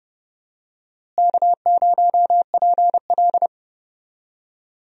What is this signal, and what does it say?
Morse code, heard as a keyed tone, about 30 wpm, 700 Hz. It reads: K0PL
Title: USA Call Signs - 4 Characters - Repeated 3x - 30wpm